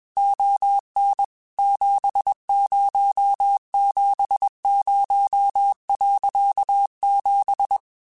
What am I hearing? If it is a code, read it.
Morse code, heard as a keyed tone, about 21 wpm, 800 Hz. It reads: ON 7070.7
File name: QSOs / QSOs-mp3s / kc7haa@20